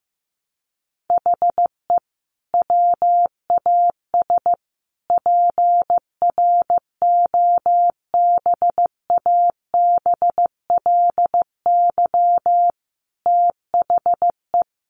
Morse code, heard as a keyed tone, about 15 wpm, 700 Hz. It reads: HE WAS PROBABLY THE
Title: Sentences from Top 1300 Words 15wpm